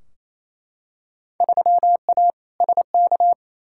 Morse code, heard as a keyed tone, about 28 wpm, 700 Hz. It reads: 3A SK
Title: Field Day Exchanges Encore - 28wpm